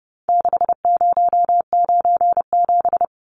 Morse code, heard as a keyed tone, about 28 wpm, 700 Hz. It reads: 6097
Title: Numbers - 4-Digit 30wpm